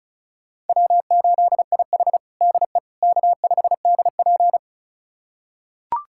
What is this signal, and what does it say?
Morse code, heard as a keyed tone, about 34 wpm, 700 Hz. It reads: W8IH DE K5DP
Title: USA Callsign DE Callsign - 4 Characters - 35wpm